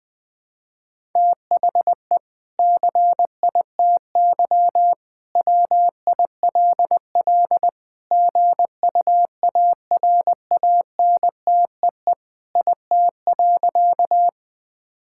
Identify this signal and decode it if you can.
Morse code, heard as a keyed tone, about 20 wpm, 700 Hz. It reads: THE CITY WILL GUARANTEE IT.
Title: Sentences from Top 1600 Words 20wpm